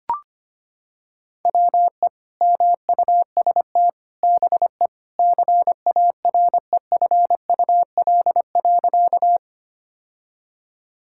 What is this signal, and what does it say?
Morse code, heard as a keyed tone, about 25 wpm, 700 Hz. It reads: WE MUST BE CAREFUL.
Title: Sentences from Top 1600 Words 25wpm